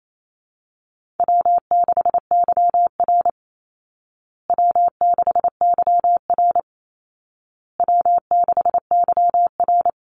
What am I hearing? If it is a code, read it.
Morse code, heard as a keyed tone, about 28 wpm, 700 Hz. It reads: W6YR W6YR W6YR
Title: Easy Call Signs Repeated 3x - 28wpm